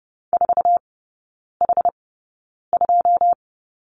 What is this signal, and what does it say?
Morse code, heard as a keyed tone, about 30 wpm, spaced as for 9 wpm, 700 Hz. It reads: 4H2
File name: Lesson 16e - Letter c - Three Characters 30wpm